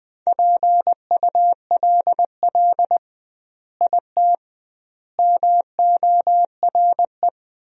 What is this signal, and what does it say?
Morse code, heard as a keyed tone, about 21 wpm, 700 Hz. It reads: PULL IT MORE
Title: 2x Word-Spacing - Sentences from Top 400 Words - 20wpm